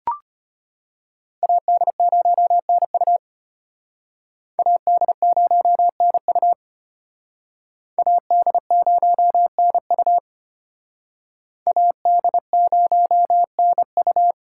Morse code, heard as a keyed tone, about 33 wpm, 700 Hz. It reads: AD0NU AD0NU AD0NU AD0NU